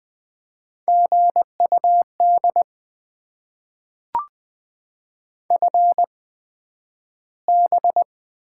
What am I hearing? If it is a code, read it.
Morse code, heard as a keyed tone, about 20 wpm, spaced as for 11 wpm, 700 Hz. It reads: GUD FB
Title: Top 20 QSO Elements - 8x Character Spacing - 20wpm